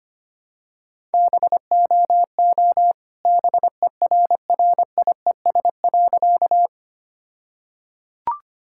Morse code, heard as a keyed tone, about 25 wpm, 700 Hz. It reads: BOO BERRIES.
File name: practice / 25wpm / All-Age Appropriate Jokes - 25wpm